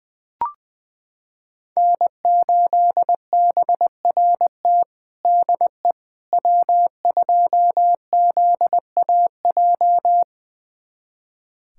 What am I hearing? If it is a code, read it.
Morse code, heard as a keyed tone, about 21 wpm, 700 Hz. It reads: N8BRT DE W2ZAJ
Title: USA Callsign DE Callsign - 5 Characters - 20wpm